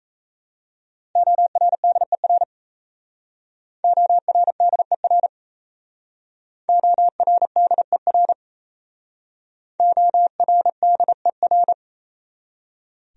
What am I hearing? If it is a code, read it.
Morse code, heard as a keyed tone, about 34 wpm, 700 Hz. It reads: ORDER ORDER ORDER ORDER
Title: Speed-Racing Top 500 Words 28wpm